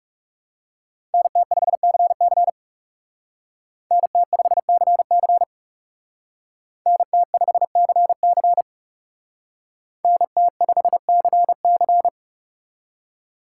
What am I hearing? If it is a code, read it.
Morse code, heard as a keyed tone, about 36 wpm, 700 Hz. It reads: NT5CC NT5CC NT5CC NT5CC
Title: Speed Racing - USA 2x2 Call Signs 30wpm